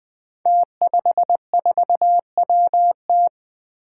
Morse code, heard as a keyed tone, about 20 wpm, 700 Hz. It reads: T54WT
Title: Five Challenging Letter-Number Combinations 20wpm